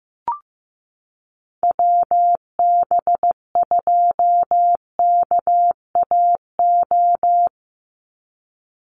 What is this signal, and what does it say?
Morse code, heard as a keyed tone, about 15 wpm, 700 Hz. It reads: WB2KAO
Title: Call Signs Repeated 3x - 15wpm